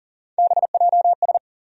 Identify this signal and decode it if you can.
Morse code, heard as a keyed tone, about 40 wpm, 700 Hz. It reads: BJS
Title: USA Call Sign Postfixes - 3 Letters 40wpm